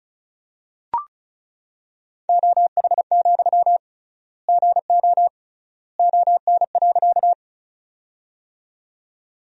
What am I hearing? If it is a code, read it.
Morse code, heard as a keyed tone, about 34 wpm, 700 Hz. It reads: OH, GO ON.
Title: 3x Word-Spacing - Sentences from Top 500 Words - 35wpm